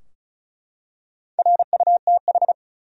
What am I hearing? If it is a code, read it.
Morse code, heard as a keyed tone, about 34 wpm, 700 Hz. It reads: RUTH
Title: USA Common Names - 35wpm